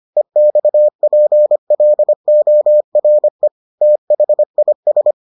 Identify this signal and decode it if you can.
Morse code, heard as a keyed tone, about 25 wpm, 600 Hz. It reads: EXPLORE THIS